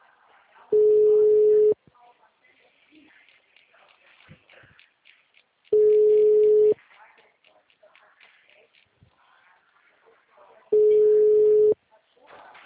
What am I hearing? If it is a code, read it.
Ringback tone (Europe, 425 Hz), with speech.
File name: out- -2031- - - .10207